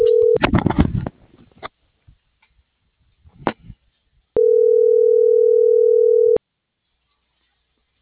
Ringback tone (North America), 440 + 480 Hz.